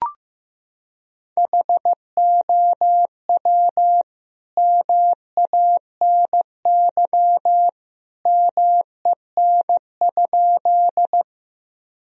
Morse code, heard as a keyed tone, about 14 wpm, 700 Hz. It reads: HOW MANY MEN?
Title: Sentences from Top 1000 Words - Review Part 1 - 15wpm